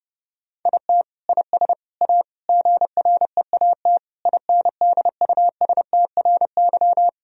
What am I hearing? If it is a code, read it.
Morse code, heard as a keyed tone, about 30 wpm, 700 Hz. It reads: IT IS A GREAT INDUSTRY
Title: Sentences from Top 1000 Words - Review Part 4 - 30wpm